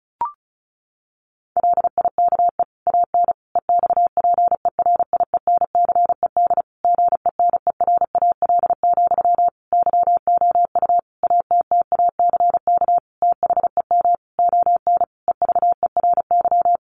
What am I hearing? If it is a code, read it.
Morse code, heard as a keyed tone, about 34 wpm, 700 Hz. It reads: LIKE AN EXPERIENCED GENERAL, YOU ATTACK THEM ON EVERY
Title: Sentences from Top 1500 Words 35wpm